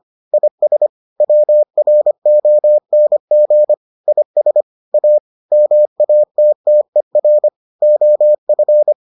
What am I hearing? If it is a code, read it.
Morse code, heard as a keyed tone, about 25 wpm, 600 Hz. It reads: IS WRONG IS A MATTER OF